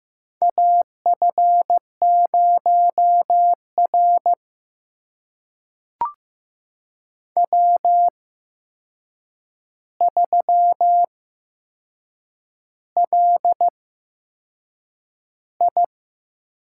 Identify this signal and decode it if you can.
Morse code, heard as a keyed tone, about 15 wpm, spaced as for 6 wpm, 700 Hz. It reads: AF0R W3LI
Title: USA Call Signs - 8x Character Spacing - 15wpm